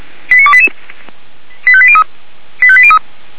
Five-tone selective call (ZVEI), heard as quick runs of five tones, tones of about 70 ms. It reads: 89288 85803 85803